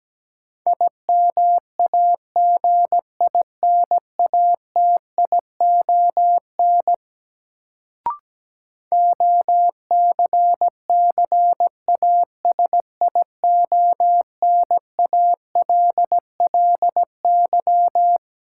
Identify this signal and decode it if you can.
Morse code, heard as a keyed tone, about 17 wpm, 700 Hz. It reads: IMAGINATION OCCASIONALLY
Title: Sentences from Top 1300 Words - New Word Drill Alt - 17wpm